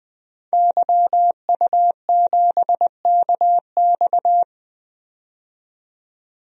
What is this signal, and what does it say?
Morse code, heard as a keyed tone, about 20 wpm, 700 Hz. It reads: YU7KX
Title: DXCC-100 Call Signs - 5 Characters - 20wpm